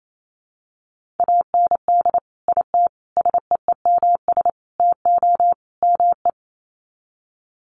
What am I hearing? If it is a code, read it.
Morse code, heard as a keyed tone, about 28 wpm, 700 Hz. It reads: AND IT SEEMS TO ME
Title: Sets of 5 Words Encore2 28wpm